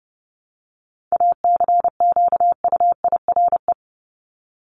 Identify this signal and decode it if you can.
Morse code, heard as a keyed tone, about 30 wpm, 700 Hz. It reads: ACQUIRE
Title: Top 1501-1750 Words 30wpm